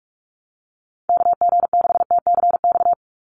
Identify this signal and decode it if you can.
Morse code, heard as a keyed tone, about 44 wpm, 700 Hz. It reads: KG6TCX
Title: USA Call Signs 45wpm